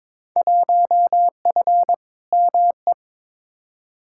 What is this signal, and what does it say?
Morse code, heard as a keyed tone, about 22 wpm, 700 Hz. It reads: 1F ME
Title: Field Day Exchanges Encore - 22wpm